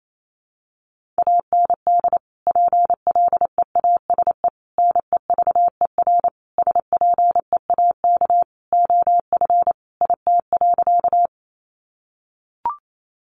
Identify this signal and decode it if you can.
Morse code, heard as a keyed tone, about 28 wpm, 700 Hz. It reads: AND PLEASE NEVER SPEAK OF IT.
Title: Sentences from Top 1500 Words 28wpm